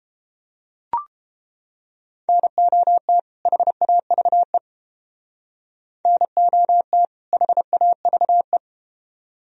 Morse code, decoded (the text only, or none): NOT HAVE NOT HAVE